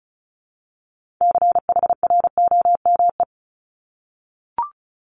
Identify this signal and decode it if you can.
Morse code, heard as a keyed tone, about 36 wpm, 700 Hz. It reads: CHROME
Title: Multi-letter Phonemes 35wpm